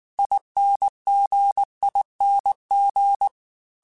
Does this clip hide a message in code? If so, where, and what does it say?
Yes: Morse code from 0.2 to 3.3 s, heard as a keyed tone, about 19 wpm, 800 Hz. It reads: INGING